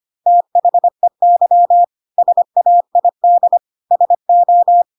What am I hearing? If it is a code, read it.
Morse code, heard as a keyed tone, about 25 wpm, 700 Hz. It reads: THEY SAID SO
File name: Pinocchio / Pinocchio ch3 25 25 ik1hge